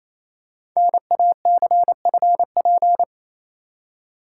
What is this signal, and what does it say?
Morse code, heard as a keyed tone, about 28 wpm, 700 Hz. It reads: NACFP
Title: Code Groups - 5 Letter:Numbers - 28wpm